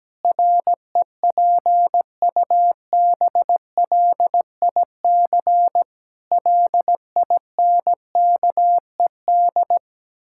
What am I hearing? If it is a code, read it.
Morse code, heard as a keyed tone, about 17 wpm, 700 Hz. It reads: REPUBLIC LINKED
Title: Pangram Challenge v2 - 17wpm